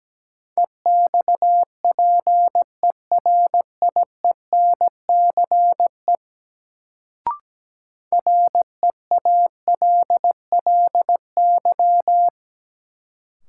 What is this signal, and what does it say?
Morse code, heard as a keyed tone, about 17 wpm, 700 Hz. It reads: EXPERIENCE REALLY
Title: Sentences from Top 600 Words - New Word Drill - 17wpm